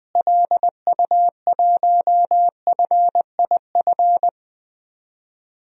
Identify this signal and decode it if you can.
Morse code, heard as a keyed tone, about 20 wpm, 700 Hz. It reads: LU1FIF
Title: DXCC-100 Call Signs - 6 Characters - Repeated 2x - 20wpm